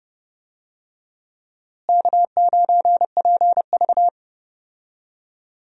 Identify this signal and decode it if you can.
Morse code, heard as a keyed tone, about 30 wpm, 700 Hz. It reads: K9PV
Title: USA Call Signs - 4 Characters - Repeated 2x - 30wpm